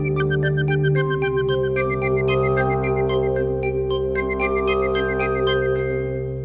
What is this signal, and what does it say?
There is no code.